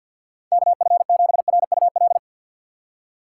Morse code, heard as a keyed tone, about 52 wpm, 700 Hz. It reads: KF7RUL
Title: USA Call Signs - 6 Characters - 4x Character Spacing - 50wpm